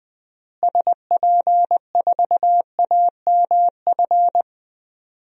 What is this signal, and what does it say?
Morse code, heard as a keyed tone, about 20 wpm, 700 Hz. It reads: SP4AMF
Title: DXCC-100 Call Signs - 6 Characters - Repeated 2x - 20wpm